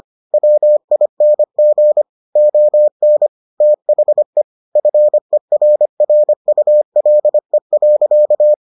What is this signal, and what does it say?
Morse code, heard as a keyed tone, about 25 wpm, 600 Hz. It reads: WING ON THE FERRULE.